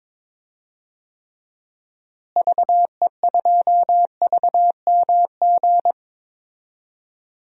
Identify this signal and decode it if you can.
Morse code, heard as a keyed tone, about 22 wpm, 700 Hz. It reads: VE2VMG
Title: DXCC-100 Call Signs - 6 Characters - Repeated 2x - 22wpm